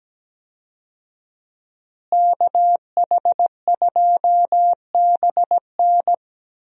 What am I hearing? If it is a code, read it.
Morse code, heard as a keyed tone, about 17 wpm, 700 Hz. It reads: KH2BN
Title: USA Call Signs - 5 Characters - Repeated 2x - 17wpm